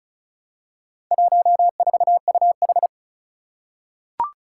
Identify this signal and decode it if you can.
Morse code, heard as a keyed tone, about 36 wpm, 700 Hz. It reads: 14UH
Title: Four Challenging Letter-Number Combinations 35wpm